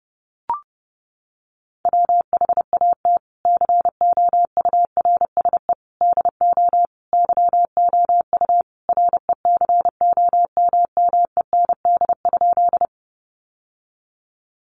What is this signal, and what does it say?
Morse code, heard as a keyed tone, about 30 wpm, 700 Hz. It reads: WHAT COURSE DO YOU RECOMMEND?
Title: Sentences from Top 1200 Words 30wpm